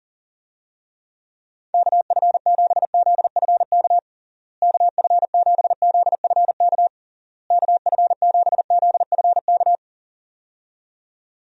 Morse code, heard as a keyed tone, about 40 wpm, 700 Hz. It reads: KF7ZFK KF7ZFK KF7ZFK
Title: USA Call Signs - 6 Characters - Repeated 3x - 40wpm